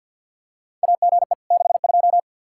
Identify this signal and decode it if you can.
Morse code, heard as a keyed tone, about 48 wpm, 700 Hz. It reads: AGE 62
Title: Practice QSO - Exchange 3 - More Info - 50wpm